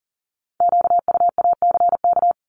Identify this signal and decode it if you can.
Morse code, heard as a keyed tone, about 41 wpm, 690 Hz. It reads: QUACK
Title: Multi-letter Phonemes 40wpm